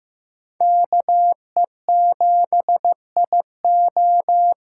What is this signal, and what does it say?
Morse code, heard as a keyed tone, about 16 wpm, 700 Hz. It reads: KE7IO
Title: USA Callsign DE Callsign - 5 Characters - 15wpm